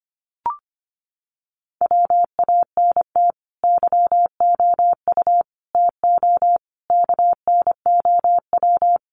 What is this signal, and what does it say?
Morse code, heard as a keyed tone, about 25 wpm, 700 Hz. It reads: WANT YOU TO KNOW
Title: Sets of 5 Words Encore 25wpm